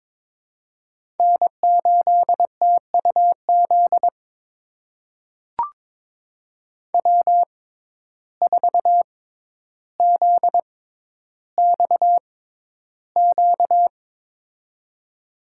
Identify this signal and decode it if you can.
Morse code, heard as a keyed tone, about 21 wpm, spaced as for 11 wpm, 700 Hz. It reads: N8TUZ W4ZXQ
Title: USA Call Signs - 5 Characters - 6x Character Spacing - 22wpm